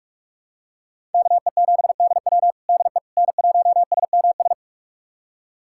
Morse code, heard as a keyed tone, about 44 wpm, 700 Hz. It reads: KE7DW DE N1SMS
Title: USA Callsign DE Callsign - 5 Characters - 45wpm